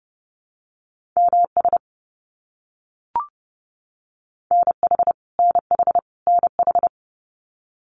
Morse code, heard as a keyed tone, about 30 wpm, 700 Hz. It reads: MS NH NH NH
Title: US States Repeated 3x 30wpm